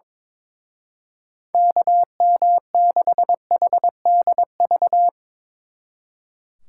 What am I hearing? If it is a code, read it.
Morse code, heard as a keyed tone, about 24 wpm, 700 Hz. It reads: KM6HDV
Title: Speed Racing - USA 2x3 Call Signs 22wpm